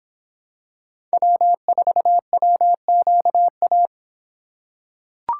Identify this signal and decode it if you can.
Morse code, heard as a keyed tone, about 26 wpm, 700 Hz. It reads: W4WQA